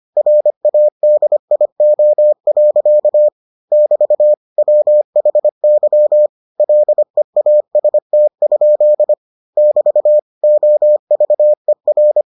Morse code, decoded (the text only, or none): RADIO. <BT> WHY LEAST? <BT> OVER